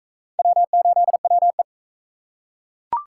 Morse code, heard as a keyed tone, about 38 wpm, 700 Hz. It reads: W8WE